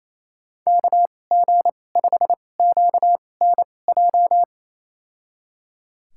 Morse code, heard as a keyed tone, about 28 wpm, spaced as for 20 wpm, 700 Hz. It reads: KG5QNJ
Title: USA Call Signs - 6 Characters - 2x Character Spacing - 28wpm